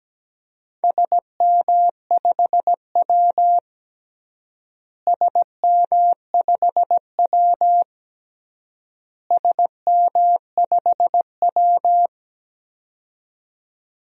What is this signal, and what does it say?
Morse code, heard as a keyed tone, about 17 wpm, 700 Hz. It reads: SM5W SM5W SM5W
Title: DXCC-100 Call Signs - 4 Characters - Repeated 3x - 17wpm